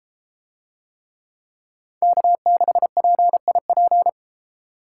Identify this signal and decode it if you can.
Morse code, heard as a keyed tone, about 31 wpm, 700 Hz. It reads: K6PIP